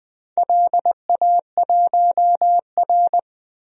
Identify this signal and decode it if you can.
Morse code, heard as a keyed tone, about 20 wpm, 700 Hz. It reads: LA1R
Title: DXCC-100 Call Signs - 4 Characters - 20wpm